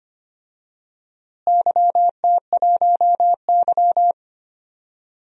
Morse code, heard as a keyed tone, about 25 wpm, 700 Hz. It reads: YT1Y
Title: DXCC-100 Call Signs - 4 Characters - Repeated 2x - 25wpm